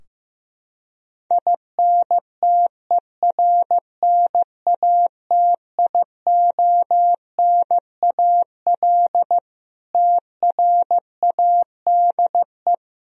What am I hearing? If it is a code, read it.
Morse code, heard as a keyed tone, about 15 wpm, 700 Hz. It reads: INTERNATIONAL TRADE
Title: Sentences from Top 1000 Words - Review Part 4 - 15wpm